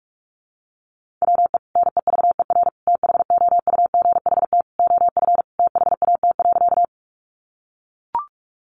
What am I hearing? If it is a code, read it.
Morse code, heard as a keyed tone, about 44 wpm, 700 Hz. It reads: WE NEVER THOUGHT OF THAT.
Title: Sentences from Top 300 Words 45wpm